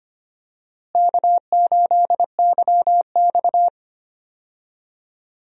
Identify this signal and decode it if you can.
Morse code, heard as a keyed tone, about 25 wpm, 700 Hz. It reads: K8YX
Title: USA Call Signs - 4 Characters - Repeated 2x - 25wpm